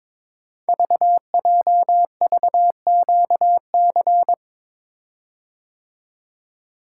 Morse code, heard as a keyed tone, about 23 wpm, 700 Hz. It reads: VJVQC
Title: Code Groups - 5 Letters - 22wpm